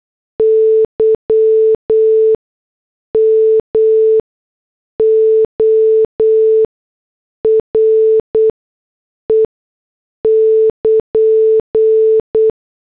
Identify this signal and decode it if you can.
Morse code, heard as a keyed tone, about 8 wpm, 440 Hz. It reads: Y M O R E <KN>
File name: morse / headlines